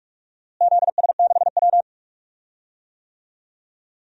Morse code, heard as a keyed tone, about 47 wpm, 700 Hz. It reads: ZS6W
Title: DXCC-100 Call Signs - 4 Characters - Repeated 3x - 45wpm